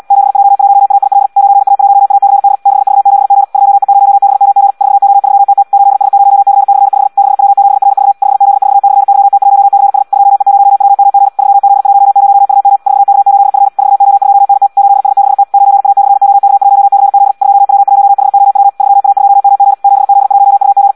Morse code, heard as a keyed tone, about 75 wpm, 800 Hz. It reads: 7J1AIL 9A1ACD 6V7G 2E0BMG 4O3TT 8S0C/5 6W7RV 5Z4/YT1CS 2E0WMG 3Z10UM 4K8F 4O2NT 8S6T 9A/OK1CZ 2W1HOW 3A2MG 7Z1UG
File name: Audio practica CW / qrq-tx-78 2